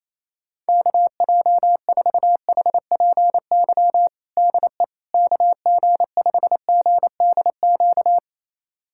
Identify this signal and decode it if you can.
Morse code, heard as a keyed tone, about 28 wpm, 700 Hz. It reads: KJ4HPY DE KG5GDQ
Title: USA Callsign DE Callsign 28wpm